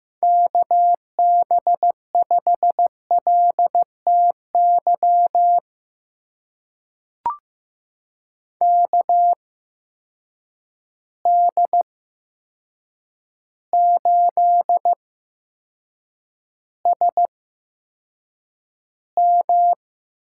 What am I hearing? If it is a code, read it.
Morse code, heard as a keyed tone, about 15 wpm, spaced as for 7 wpm, 700 Hz. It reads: KB5LTY KD8SM
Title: USA Call Signs - 6 Characters - 8x Character Spacing - 15wpm